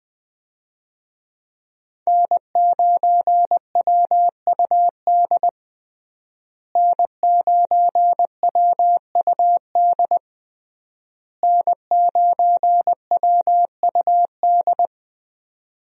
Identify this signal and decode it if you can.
Morse code, heard as a keyed tone, about 20 wpm, 700 Hz. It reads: N9WUD N9WUD N9WUD
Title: USA Call Signs - 5 Characters - Repeated 3x - 20wpm